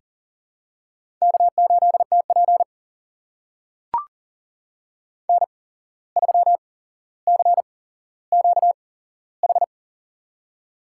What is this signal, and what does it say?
Morse code, heard as a keyed tone, about 40 wpm, spaced as for 16 wpm, 700 Hz. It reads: K8TP N3CQH
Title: USA Call Signs - 8x Character Spacing - 40wpm